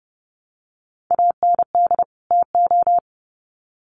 Morse code, heard as a keyed tone, about 30 wpm, 700 Hz. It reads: AND TO
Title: Sets of 2 Words 30wpm